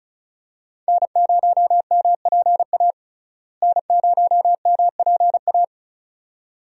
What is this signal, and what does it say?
Morse code, heard as a keyed tone, about 34 wpm, 700 Hz. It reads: N0MPA N0MPA
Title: USA Call Signs - 5 Characters - Repeated 2x - 35wpm